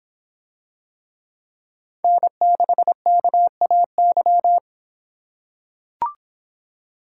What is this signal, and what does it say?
Morse code, heard as a keyed tone, about 26 wpm, 700 Hz. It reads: N6KAY